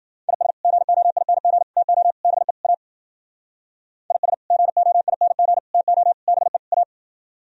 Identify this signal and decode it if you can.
Morse code, heard as a keyed tone, about 52 wpm, 700 Hz. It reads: IS GOING TO BE A IS GOING TO BE A